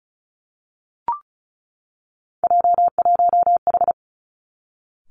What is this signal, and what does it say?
Morse code, heard as a keyed tone, about 36 wpm, 700 Hz. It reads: J1H
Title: Three Challenging Letter-Number Combinations 35wpm